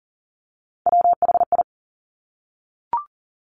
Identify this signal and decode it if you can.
Morse code, heard as a keyed tone, about 40 wpm, 700 Hz. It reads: WHI
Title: Top Three Letter Combinations in Words 40wpm